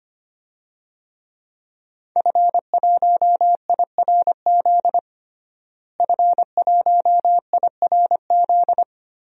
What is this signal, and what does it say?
Morse code, heard as a keyed tone, about 25 wpm, 700 Hz. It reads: F1IRZ F1IRZ
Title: DXCC-100 Call Signs - 5 Characters - Repeated 3x - 25wpm